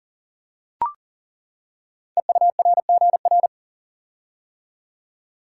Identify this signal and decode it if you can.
Morse code, heard as a keyed tone, about 38 wpm, 700 Hz. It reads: EURGR
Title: Code Groups - 5 Letters - 40wpm